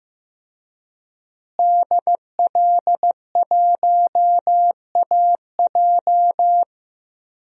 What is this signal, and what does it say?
Morse code, heard as a keyed tone, about 15 wpm, 700 Hz. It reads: DL1AJ
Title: DXCC-100 Call Signs - 5 Characters - Repeated 3x - 15wpm